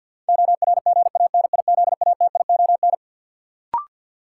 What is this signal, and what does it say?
Morse code, heard as a keyed tone, about 52 wpm, 700 Hz. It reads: ORGANIZATION
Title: Sentences from Top 1000 Words - Review Part 5 - New Word Drill - 50wpm